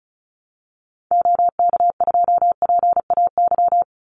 Morse code, heard as a keyed tone, about 34 wpm, 700 Hz. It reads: OK2PAY
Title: Call Signs Repeated 3x - 35wpm